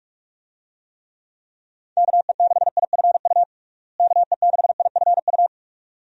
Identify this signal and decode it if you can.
Morse code, heard as a keyed tone, about 46 wpm, 700 Hz. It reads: KE6IFU KE6IFU
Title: USA Call Signs - 6 Characters - Repeated 2x - 45wpm